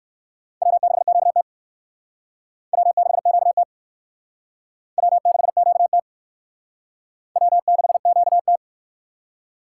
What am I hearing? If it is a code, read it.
Morse code, heard as a keyed tone, about 57 wpm, 700 Hz. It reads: W6QT W6QT W6QT W6QT